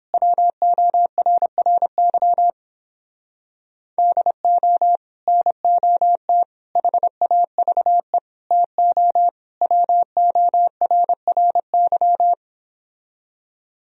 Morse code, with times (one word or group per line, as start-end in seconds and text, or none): 0.1-2.5 WORRY
4.0-5.0 DO
5.3-6.4 NOT
6.8-8.2 HAVE
8.5-9.3 TO
9.6-12.3 WORRY